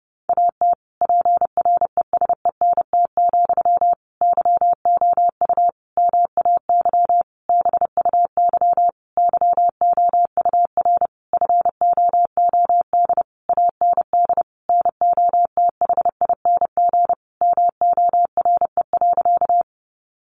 Morse code, heard as a keyed tone, about 30 wpm, 700 Hz. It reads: AT PRESENT, YOU MAY BUY YOUR FOOD AND NOTHING MORE.
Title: Sentences from Top 800 Words 30wpm